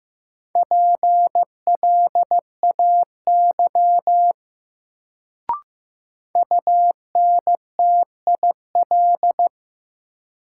Morse code, heard as a keyed tone, about 15 wpm, 700 Hz. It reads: PLAY UNTIL
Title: Sets of 5 Words Encore - New Word Drill - 15wpm